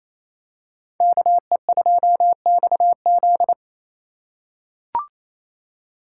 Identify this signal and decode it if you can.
Morse code, heard as a keyed tone, about 28 wpm, 700 Hz. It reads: KE2XZ
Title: USA Call Signs - 6x Character Spacing - 28wpm